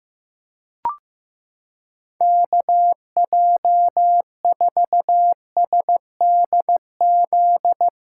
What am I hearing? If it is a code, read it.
Morse code, heard as a keyed tone, about 16 wpm, 700 Hz. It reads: KJ4SDZ
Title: USA Callsign DE Callsign - 6 Characters - 15wpm